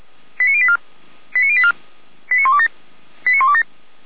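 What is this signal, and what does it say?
Five-tone selective call (ZVEI), heard as quick runs of five tones, tones of about 70 ms. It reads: 89084 89084 89127 89127
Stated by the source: ZVEI five-tone selective call 89520 received on an SDR.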